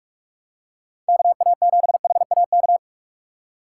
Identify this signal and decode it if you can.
Morse code, heard as a keyed tone, about 46 wpm, 700 Hz. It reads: KA7HAK
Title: USA Call Signs - 3x Character Spacing - 45wpm